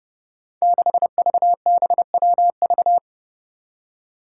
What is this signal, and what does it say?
Morse code, heard as a keyed tone, about 30 wpm, 700 Hz. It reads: 6VBWV
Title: Five Challenging Letter-Number Combinations 30wpm